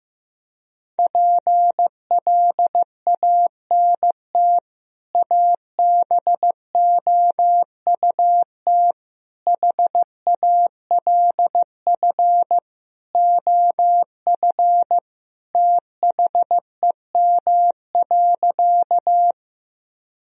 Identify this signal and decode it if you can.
Morse code, heard as a keyed tone, about 15 wpm, 700 Hz. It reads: PLANT ABOUT HALF OF THEM.
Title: Sentences from Top 1000 Words - Review Part 1 - 15wpm